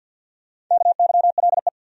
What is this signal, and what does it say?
Morse code, heard as a keyed tone, about 53 wpm, 700 Hz. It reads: KYLE
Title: USA Common Names - Max 4 Letters - 50wpm